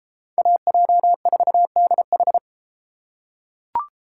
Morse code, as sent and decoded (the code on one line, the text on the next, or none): .- .--- ....- -.. ....
AJ4DH